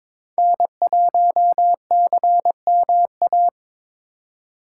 Morse code, heard as a keyed tone, about 22 wpm, 700 Hz. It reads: N1CMA
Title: USA Call Signs - 5 Characters - 4x Character Spacing - 22wpm